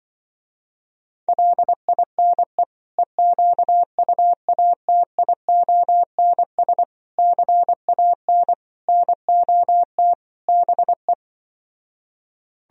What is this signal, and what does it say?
Morse code, heard as a keyed tone, about 25 wpm, 700 Hz. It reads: LINE EQUATIONS CAN NOT BE